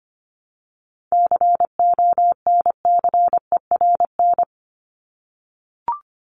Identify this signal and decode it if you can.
Morse code, heard as a keyed tone, about 25 wpm, 700 Hz. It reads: CONCERN
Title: Top 751-1000 Words 25wpm